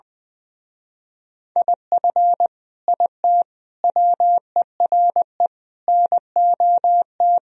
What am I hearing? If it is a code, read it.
Morse code, heard as a keyed tone, about 20 wpm, 700 Hz. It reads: IF IT WERE NOT
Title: Speed-Racing - Sets of 5 Words 20wpm